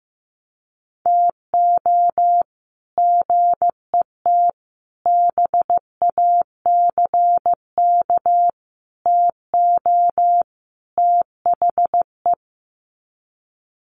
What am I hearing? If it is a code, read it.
Morse code, heard as a keyed tone, about 15 wpm, 700 Hz. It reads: TO GET BACK TO THE
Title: Sets of 5 Words Encore 15wpm